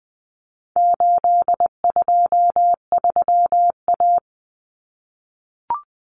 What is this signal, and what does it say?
Morse code, heard as a keyed tone, about 20 wpm, 700 Hz. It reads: 823A
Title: Common Cut Numbers - 4-Digit 20wpm